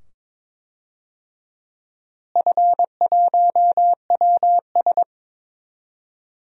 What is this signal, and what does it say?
Morse code, heard as a keyed tone, about 22 wpm, 700 Hz. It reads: F1WS
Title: DXCC-100 Call Signs - 4 Characters - Repeated 2x - 22wpm